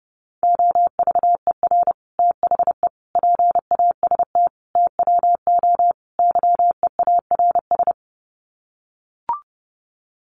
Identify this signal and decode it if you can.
Morse code, heard as a keyed tone, about 28 wpm, 700 Hz. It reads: OVER THE PAST TWO YEARS
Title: Sets of 5 Words Encore 30wpm